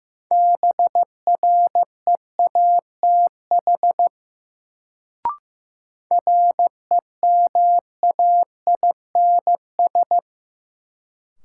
Morse code, heard as a keyed tone, about 15 wpm, 700 Hz. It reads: BREATH REMAINS
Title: Sets of 5 Words Encore - New Word Drill - 15wpm